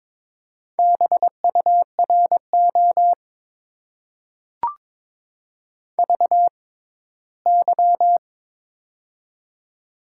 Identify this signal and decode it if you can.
Morse code, heard as a keyed tone, about 22 wpm, spaced as for 15 wpm, 700 Hz. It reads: BURO VY
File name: Top 90 QSO Elements - 6x Character Spacing - 22wpm